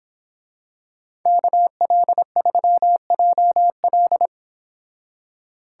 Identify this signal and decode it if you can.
Morse code, heard as a keyed tone, about 25 wpm, 700 Hz. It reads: KL3JL
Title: Speed Racing - USA 2x2 Call Signs 22wpm